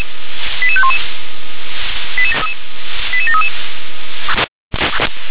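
Five-tone selective call (ZVEI), heard as quick runs of five tones, tones of about 70 ms. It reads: 88511 88733
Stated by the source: Received on an SDR.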